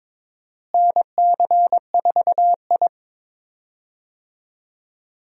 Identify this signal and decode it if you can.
Morse code, heard as a keyed tone, about 23 wpm, 700 Hz. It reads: NC4I
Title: USA Call Signs - 4 Characters - Repeated 2x - 22wpm